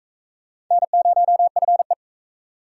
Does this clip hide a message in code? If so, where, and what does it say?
Yes: Morse code from 0.7 to 1.9 s, heard as a keyed tone, about 40 wpm, 700 Hz. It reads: N0FE